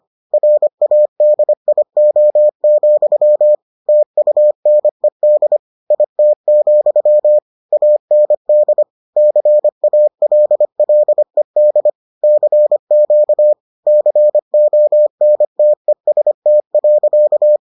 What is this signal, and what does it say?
Morse code, heard as a keyed tone, about 25 wpm, 600 Hz. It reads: RADIO, TUNED IT, AND CALLED CQ CONTEST.